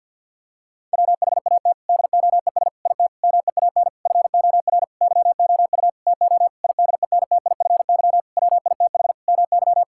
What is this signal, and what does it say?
Morse code, heard as a keyed tone, about 50 wpm, 700 Hz. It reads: WHAT DOES IT MEAN FOR YOU TO IDENTIFY WITH MY
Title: Sentences from Top 1000 Words - Review Part 6 - 50wpm